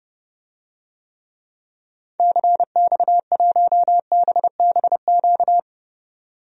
Morse code, heard as a keyed tone, about 30 wpm, 700 Hz. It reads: CX1BBQ
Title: DXCC-100 Call Signs - 6 Characters - 30wpm